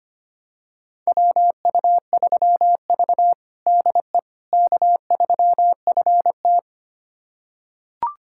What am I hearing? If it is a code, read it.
Morse code, heard as a keyed tone, about 25 wpm, 700 Hz. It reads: WU3V DE K3FT
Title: USA Callsign DE Callsign - 4 Characters - 25wpm